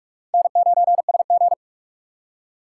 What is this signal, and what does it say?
Morse code, heard as a keyed tone, about 40 wpm, 700 Hz. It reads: N9SG